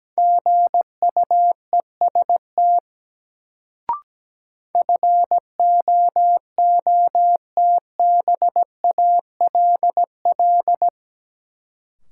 Morse code, heard as a keyed tone, about 17 wpm, 700 Hz. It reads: GUEST FOOTBALL
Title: Sentences from Top 1600 Words - New Word Drill Alt - 17wpm